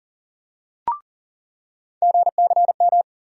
Morse code, heard as a keyed tone, about 40 wpm, 700 Hz. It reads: GCM
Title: USA Call Sign Postfixes - 3 Letters 40wpm